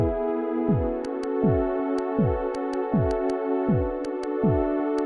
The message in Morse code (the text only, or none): none